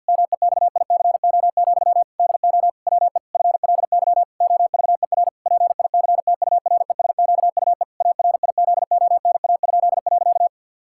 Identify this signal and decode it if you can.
Morse code, heard as a keyed tone, about 50 wpm, 700 Hz. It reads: MEXICO, DO WE FLY OVER PICTURESQUE ARIZONA?.
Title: Pangram Challenge v2 - 50wpm